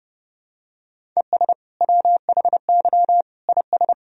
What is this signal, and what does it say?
Morse code, heard as a keyed tone, about 32 wpm, 700 Hz. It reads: ES WHY IS
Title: Sentences from Top 200 Words and Abbreviations 30wpm